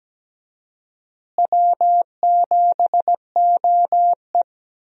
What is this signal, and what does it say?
Morse code, heard as a keyed tone, about 17 wpm, 700 Hz. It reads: W7OE
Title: USA Call Signs - 4 Characters - Repeated 3x - 17wpm